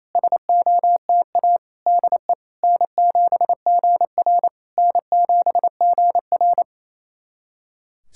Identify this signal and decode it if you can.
Morse code, heard as a keyed tone, about 28 wpm, 700 Hz. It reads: SOTA DE N7GR N7GR
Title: SOTA QSO Practice 28wpm